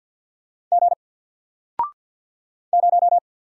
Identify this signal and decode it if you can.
Morse code, heard as a keyed tone, about 50 wpm, 700 Hz. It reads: G 0
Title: Single Character (Letter:Number:Punctuation:Prosign) Alt 50wpm